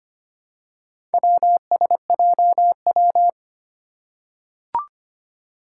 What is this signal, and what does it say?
Morse code, heard as a keyed tone, about 25 wpm, 700 Hz. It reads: WSJW
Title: Four Challenging Letter-Number Combinations 25wpm